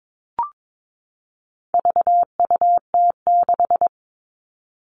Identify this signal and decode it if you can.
Morse code, heard as a keyed tone, about 22 wpm, 700 Hz. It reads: VUT6
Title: All Cut Numbers - 4-Digit 22wpm